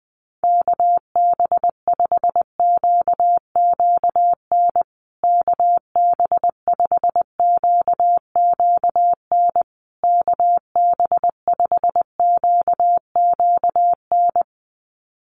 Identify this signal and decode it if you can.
Morse code, heard as a keyed tone, about 20 wpm, 700 Hz. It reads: KB5QQN KB5QQN KB5QQN
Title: USA Call Signs Repeated 3x 20wpm